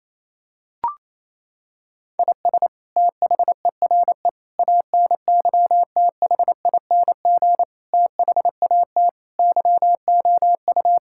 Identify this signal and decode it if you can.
Morse code, heard as a keyed tone, about 28 wpm, 700 Hz. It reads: IS THERE ANYTHING THAT YOU
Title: Sentences from Top 1000 Words - Review Part 6 - 28wpm